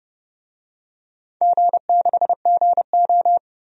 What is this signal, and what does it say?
Morse code, heard as a keyed tone, about 30 wpm, 700 Hz. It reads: G6GO
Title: DXCC-100 Call Signs - 4 Characters - 30wpm